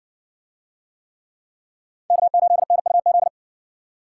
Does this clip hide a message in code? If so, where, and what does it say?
Yes: Morse code from 2.1 to 3.3 s, heard as a keyed tone, about 60 wpm, 700 Hz. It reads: K8NUZ